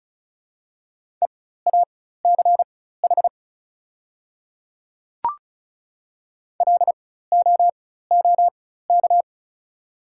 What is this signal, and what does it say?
Morse code, heard as a keyed tone, about 36 wpm, spaced as for 16 wpm, 700 Hz. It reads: EACH LOOK
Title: 4x Character Spacing - Top 100 Words - 35wpm